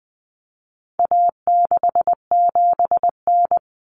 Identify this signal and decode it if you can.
Morse code, heard as a keyed tone, about 20 wpm, 700 Hz. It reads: A67N
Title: Common Cut Numbers - 4-Digit 20wpm